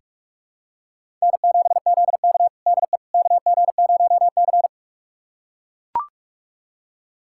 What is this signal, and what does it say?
Morse code, heard as a keyed tone, about 41 wpm, 700 Hz. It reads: N7ZK DE KG0C
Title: USA Callsign DE Callsign - 4 Characters - 45wpm